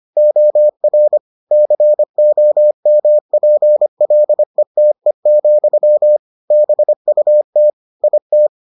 Morse code, heard as a keyed tone, about 25 wpm, 600 Hz. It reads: OR COMPLETE, BUT IT